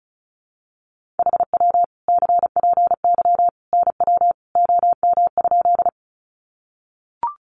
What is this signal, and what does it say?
Morse code, heard as a keyed tone, about 34 wpm, 700 Hz. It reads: HW CPY NW OM?
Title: Common QSO Phrases 35wpm